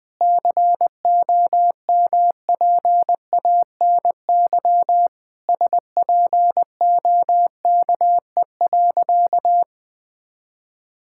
Morse code, heard as a keyed tone, about 20 wpm, 700 Hz. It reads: COMPANY SPOKE.
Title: Sentences with 3-Word Phrases - 20wpm